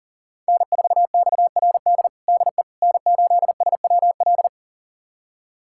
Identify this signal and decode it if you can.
Morse code, heard as a keyed tone, about 40 wpm, 700 Hz. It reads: N4XRD DE N8SWL
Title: USA Callsign DE Callsign 40wpm